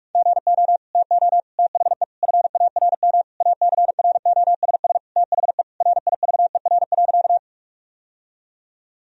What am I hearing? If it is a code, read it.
Morse code, heard as a keyed tone, about 41 wpm, 700 Hz. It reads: GO TO THE FARM ACROSS THE RIVER.
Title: Sentences from Top 1000 Words - Review Part 2 - 45wpm